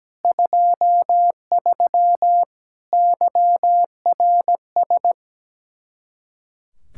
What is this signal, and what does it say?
Morse code, heard as a keyed tone, about 17 wpm, 700 Hz. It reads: 23 YRS
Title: Speed-Racing Common QSO Phrases - 17wpm